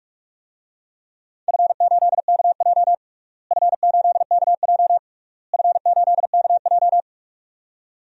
Morse code, heard as a keyed tone, about 46 wpm, 700 Hz. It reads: F8KJ F8KJ F8KJ
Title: DXCC-100 Call Signs - 4 Characters - Repeated 3x - 45wpm